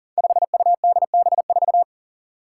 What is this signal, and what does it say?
Morse code, heard as a keyed tone, about 39 wpm, 700 Hz. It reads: 5UDB4
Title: Five Challenging Letter-Number Combinations 40wpm